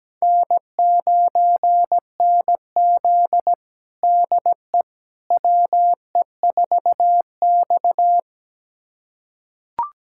Morse code, heard as a keyed tone, about 17 wpm, 700 Hz. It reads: N9NZ DE WE4X
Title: USA Callsign DE Callsign - 4 Characters - 17wpm